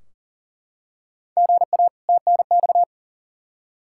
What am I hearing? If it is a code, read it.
Morse code, heard as a keyed tone, about 40 wpm, 700 Hz. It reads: GA TNX
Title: K1USN SST QSO Practice 40wpm